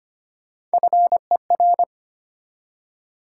Morse code, heard as a keyed tone, about 25 wpm, 700 Hz. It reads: FER
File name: Top 20 QSO Elements - 4x Character Spacing - 25wpm